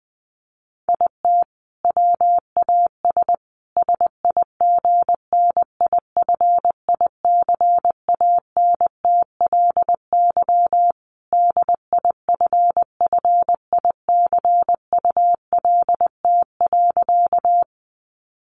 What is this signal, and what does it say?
Morse code, heard as a keyed tone, about 20 wpm, 700 Hz. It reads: IT WAS SIGNIFICANTLY DIFFICULT.
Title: Sentences from Top 1500 Words 20wpm